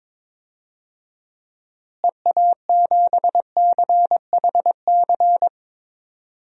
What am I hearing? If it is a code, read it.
Morse code, heard as a keyed tone, about 21 wpm, 700 Hz. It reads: EA7CHC
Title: DXCC-100 Call Signs - 6 Characters - Repeated 3x - 22wpm